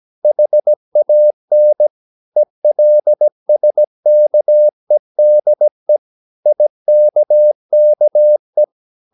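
Morse code, heard as a keyed tone, about 17 wpm, 600 Hz. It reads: HAN ELSKEDE IKKE
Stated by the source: Slow (QRS) Morse practice at 17 wpm, from CW med Gnister.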